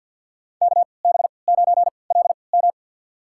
Morse code, heard as a keyed tone, about 52 wpm, spaced as for 29 wpm, 700 Hz. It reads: KB9LM
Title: USA Call Signs - 5 Characters - 3x Character Spacing - 50wpm